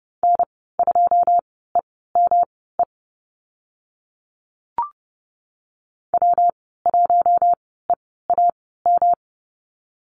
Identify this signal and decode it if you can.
Morse code, heard as a keyed tone, about 28 wpm, spaced as for 17 wpm, 700 Hz. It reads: N2EME W1EAM
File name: Lesson 19g - Letter w - Call Signs 30wpm